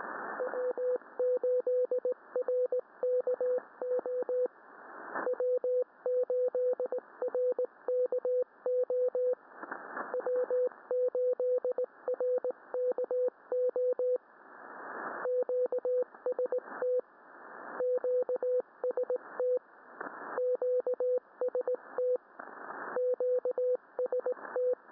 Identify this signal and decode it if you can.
Morse code, heard as a keyed tone, about 19 wpm, 500 Hz. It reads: W8RKO W8RKO W8RKO QST QST QST QST